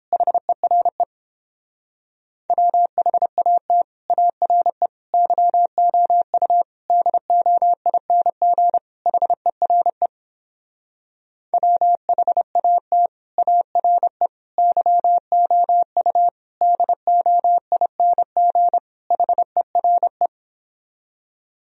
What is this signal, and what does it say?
Morse code, heard as a keyed tone, about 29 wpm, 700 Hz. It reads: HERE WHAT ARE YOU DOING HERE WHAT ARE YOU DOING HERE